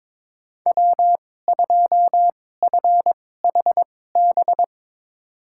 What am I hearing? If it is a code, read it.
Morse code, heard as a keyed tone, about 22 wpm, spaced as for 16 wpm, 700 Hz. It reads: W2FHB
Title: USA Call Signs - 5 Characters - 2x Character Spacing - 22wpm